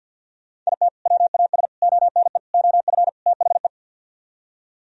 Morse code, heard as a keyed tone, about 64 wpm, 700 Hz. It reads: IT WAS ONE OF THE